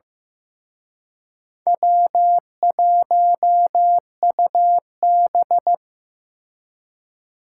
Morse code, heard as a keyed tone, about 18 wpm, 700 Hz. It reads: W1UB
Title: Speed Racing - USA Call Signs 15wpm